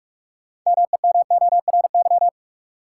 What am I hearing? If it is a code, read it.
Morse code, heard as a keyed tone, about 46 wpm, 700 Hz. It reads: MEMORY
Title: Sentences from Top 1300 Words - New Word Drill Alt - 45wpm